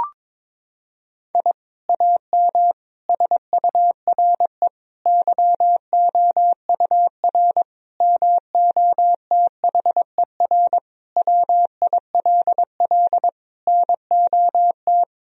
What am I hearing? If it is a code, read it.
Morse code, heard as a keyed tone, about 22 wpm, 700 Hz. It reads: I AM SURE YOUR MOTHER WILL NOT
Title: Sentences from Top 1000 Words - Review Part 3 - 22wpm